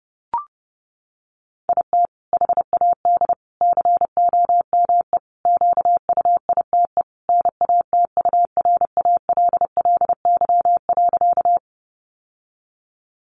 Morse code, heard as a keyed tone, about 30 wpm, 700 Hz. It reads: IT HAD COME QUITE NATURALLY.
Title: Sentences from Top 1300 Words 30wpm